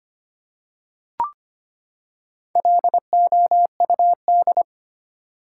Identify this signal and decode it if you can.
Morse code, heard as a keyed tone, about 25 wpm, 700 Hz. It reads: LOUD
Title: Four Letter Words - 25wpm